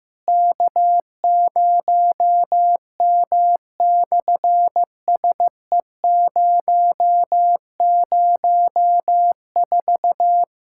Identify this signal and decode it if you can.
Morse code, heard as a keyed tone, about 15 wpm, 700 Hz. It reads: K0M/SE004
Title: SOTA QSO Practice 15wpm